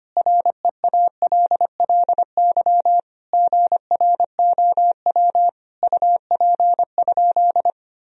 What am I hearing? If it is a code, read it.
Morse code, heard as a keyed tone, about 25 wpm, 700 Hz. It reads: REALLY GROW UP?
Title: Sentences from Top 1000 Words - Review Part 1 - 25wpm